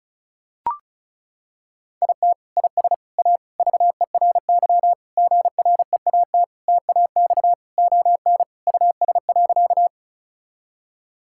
Morse code, heard as a keyed tone, about 34 wpm, 700 Hz. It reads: IT IS A VERY GREAT TAX ON US.
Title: Sentences from Top 1000 Words - Review Part 5 - 35wpm